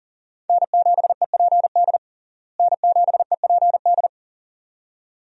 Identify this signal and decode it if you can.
Morse code, heard as a keyed tone, about 40 wpm, 700 Hz. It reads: N7EPD N7EPD
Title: USA Call Signs - 5 Characters - Repeated 2x - 40wpm